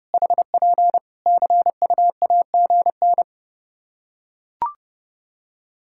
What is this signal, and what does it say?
Morse code, heard as a keyed tone, about 30 wpm, 700 Hz. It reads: HP CUAGN
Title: Practice QSO - Exchange 4 - Ending - 30wpm